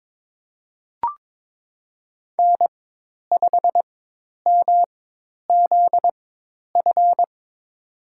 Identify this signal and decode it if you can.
Morse code, heard as a keyed tone, about 22 wpm, spaced as for 10 wpm, 700 Hz. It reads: N5MZF
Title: USA Call Signs - 5 Characters - 4x Character Spacing - 22wpm